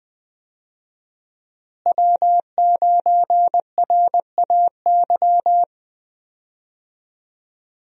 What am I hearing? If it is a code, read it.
Morse code, heard as a keyed tone, about 20 wpm, 700 Hz. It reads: W9RAY
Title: USA Call Signs - 5 Characters - Repeated 2x - 20wpm